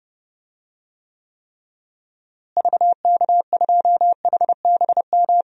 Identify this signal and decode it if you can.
Morse code, heard as a keyed tone, about 30 wpm, 700 Hz. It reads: VK2HBM
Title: DXCC-100 Call Signs - 6 Characters - Repeated 3x - 30wpm